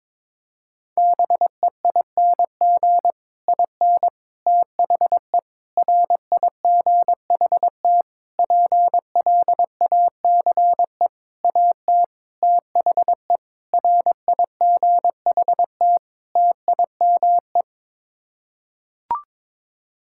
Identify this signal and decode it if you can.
Morse code, heard as a keyed tone, about 22 wpm, 700 Hz. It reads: BEING IN THE RIGHT PLACE AT THE RIGHT TIME
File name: Sets of 9 Words - 22wpm